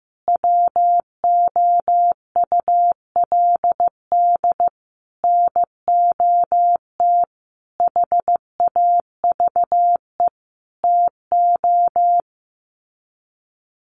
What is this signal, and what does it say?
Morse code, heard as a keyed tone, about 15 wpm, 700 Hz. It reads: WOULD NOT HAVE TO
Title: Sets of 5 Words Encore 15wpm